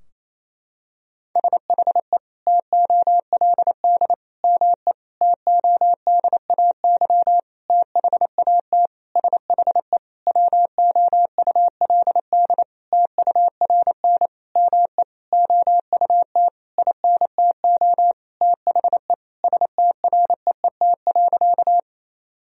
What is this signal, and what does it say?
Morse code, heard as a keyed tone, about 28 wpm, 700 Hz. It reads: SHE TOLD ME TODAY THAT SHE WOULD TURN ME OUT INTO THE STREET.
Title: Sentences from Top 1000 Words - Review Part 4 - 28wpm